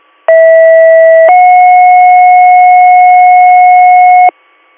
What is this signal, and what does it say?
Two-tone page, heard as a one-second tone, then a three-second tone, about 650 then 730 Hz, with noise. It is heard from 0.3 to 4.3 s.